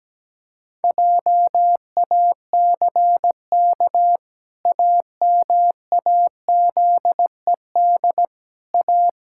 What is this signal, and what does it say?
Morse code, heard as a keyed tone, about 17 wpm, 700 Hz. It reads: JACK AMAZED A
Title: Pangram Challenge v2 - 17wpm